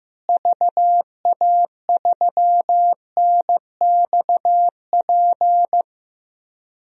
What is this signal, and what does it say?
Morse code, heard as a keyed tone, about 15 wpm, 700 Hz. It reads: VA3NXP
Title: DXCC-100 Call Signs - 6 Characters - Repeated 2x - 15wpm